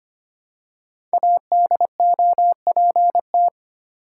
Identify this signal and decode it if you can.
Morse code, heard as a keyed tone, about 25 wpm, 700 Hz. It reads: ADOPT
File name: 5 Letter Words - First 3 Letters Given - 25wpm